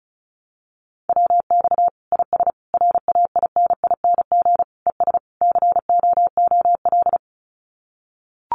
Morse code, heard as a keyed tone, about 34 wpm, 700 Hz. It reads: WX IS RAINING ES COOL
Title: Common QSO Phrases 35wpm